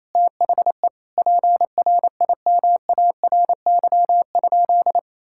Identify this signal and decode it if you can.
Morse code, heard as a keyed tone, about 28 wpm, 700 Hz. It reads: THE PRIMARY?
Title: Sentences from Top 1000 Words - Review Part 5 - 28wpm